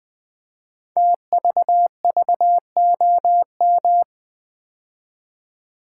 Morse code, heard as a keyed tone, about 20 wpm, 700 Hz. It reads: TVVOM
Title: Code Groups - 5 Letters - 20wpm